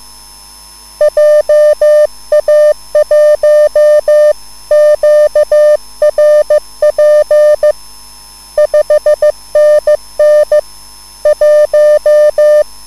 Morse code, heard as a keyed tone, about 15 wpm, 600 Hz. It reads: JA1QRP 5NN 1
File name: QSL / CW lesson